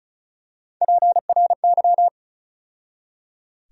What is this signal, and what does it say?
Morse code, heard as a keyed tone, about 35 wpm, 700 Hz. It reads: PRY
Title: USA Call Sign Postfixes - 3 Letters 35wpm